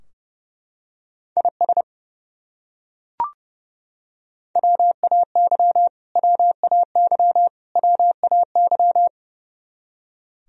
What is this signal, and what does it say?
Morse code, heard as a keyed tone, about 30 wpm, 700 Hz. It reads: IS WAY WAY WAY
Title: Top 100 Words Repeated 30wpm